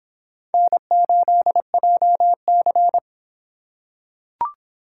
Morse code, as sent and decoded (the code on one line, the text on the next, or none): -. ---.. .--- -.-.
N8JC